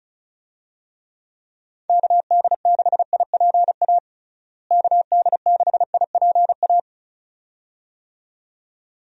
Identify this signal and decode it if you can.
Morse code, heard as a keyed tone, about 34 wpm, 700 Hz. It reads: KD6IPA KD6IPA
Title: USA Call Signs - 6 Characters - Repeated 2x - 35wpm